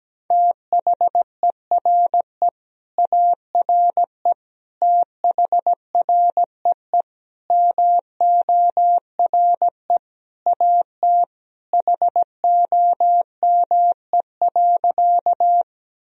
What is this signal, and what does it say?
Morse code, heard as a keyed tone, about 17 wpm, 700 Hz. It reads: THERE ARE THREE MORE AT HOME.
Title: Sentences from Top 1000 Words - Review Part 1 - 17wpm